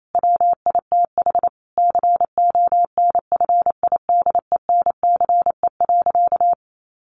Morse code, heard as a keyed tone, about 28 wpm, 700 Hz. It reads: WITH CONFIDENCE.
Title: Sentences from Top 1500 Words 28wpm